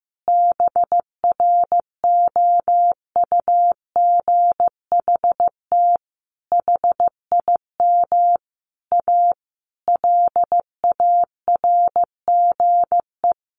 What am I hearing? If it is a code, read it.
Morse code, heard as a keyed tone, about 15 wpm, 700 Hz. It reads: BROUGHT HIM A LARGE
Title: Sentences from Top 1100 Words 15wpm